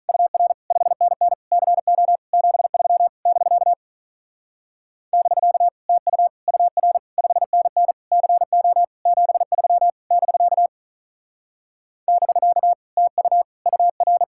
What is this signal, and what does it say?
Morse code, heard as a keyed tone, about 43 wpm, 700 Hz. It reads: UR 5NN CO 73 <BK> <BK> TU UR 5NN CO 73 <BK> <BK> TU UR
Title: Speed-Racing Common QSO Phrases - 35wpm